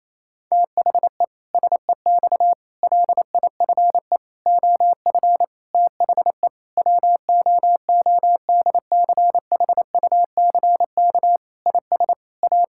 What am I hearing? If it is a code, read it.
Morse code, heard as a keyed tone, about 28 wpm, 700 Hz. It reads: THE SEX LIFE OF THE WOODCHUCK IS A
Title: Pangram Challenge v2 - 28wpm